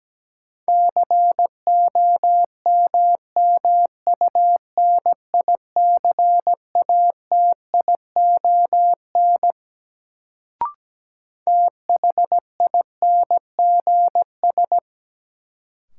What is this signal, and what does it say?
Morse code, heard as a keyed tone, about 17 wpm, 700 Hz. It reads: COMMUNICATION THINGS
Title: Sentences from Top 1400 Words - New Word Drill Alt - 17wpm